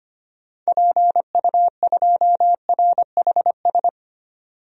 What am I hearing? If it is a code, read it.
Morse code, heard as a keyed tone, about 25 wpm, 690 Hz. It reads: PU2RHS
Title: DXCC-100 Call Signs - 6 Characters - 25wpm